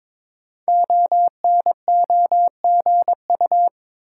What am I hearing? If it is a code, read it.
Morse code, heard as a keyed tone, about 22 wpm, 700 Hz. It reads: ONOGU
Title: Code Groups - 5 Letters - 22wpm